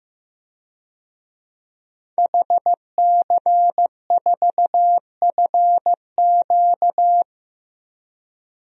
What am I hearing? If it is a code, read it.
Morse code, heard as a keyed tone, about 15 wpm, 700 Hz. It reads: HC4FQ
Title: DXCC-100 Call Signs - 5 Characters - 15wpm